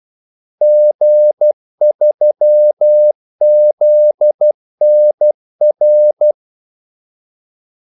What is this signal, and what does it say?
Morse code, heard as a keyed tone, about 12 wpm, 600 Hz. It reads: G3ZNR
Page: 2 ekstra ordmellemrum mellem kaldesignalerne